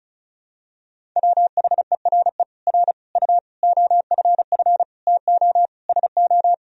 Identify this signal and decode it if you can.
Morse code, heard as a keyed tone, about 36 wpm, 700 Hz. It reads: WHERE R U OFF TO SO
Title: Sentences from Top 400 Words and Abbreviations 35wpm